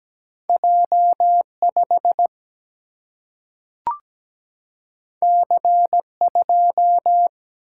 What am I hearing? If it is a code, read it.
Morse code, heard as a keyed tone, about 17 wpm, 700 Hz. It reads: J 5 C 2
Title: Single-Letter Single-Number 17wpm